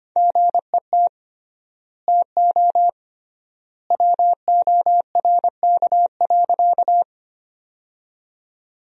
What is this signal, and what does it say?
Morse code, heard as a keyed tone, about 25 wpm, 700 Hz. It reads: GET TO WORK.
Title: 3x Word-Spacing - Sentences from Top 200 Words - 25wpm